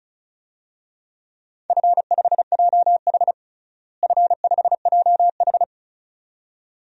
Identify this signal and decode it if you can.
Morse code, heard as a keyed tone, about 34 wpm, 700 Hz. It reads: F5JH F5JH
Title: DXCC-100 Call Signs - 4 Characters - Repeated 2x - 35wpm